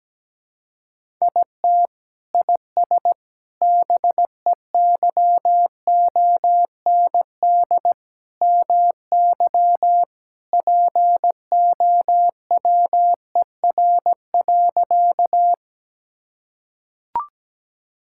Morse code, heard as a keyed tone, about 17 wpm, 700 Hz. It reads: IT IS BEYOND MY POWER.
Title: Sentences from Top 1000 Words - Review Part 6 - 17wpm